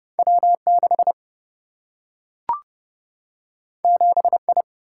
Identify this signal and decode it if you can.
Morse code, heard as a keyed tone, about 30 wpm, 700 Hz. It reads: W6 7I
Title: Single-Letter Single-Number 30wpm